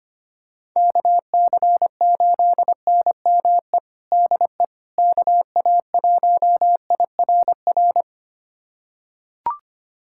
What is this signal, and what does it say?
Morse code, heard as a keyed tone, about 25 wpm, 700 Hz. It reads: KC8NME DE KA1IRR
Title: USA Callsign DE Callsign - 6 Characters - 25wpm